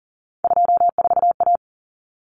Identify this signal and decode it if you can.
Morse code, heard as a keyed tone, about 40 wpm, 700 Hz. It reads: 24A
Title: Common Cut Numbers - 3-Digit 40wpm